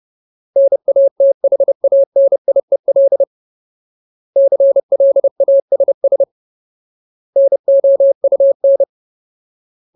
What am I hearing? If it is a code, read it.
Morse code, heard as a keyed tone, about 30 wpm, 550 Hz. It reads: NATHANIEL CLASS NOUN
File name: Ultimate Word List / 30WPM Ultimate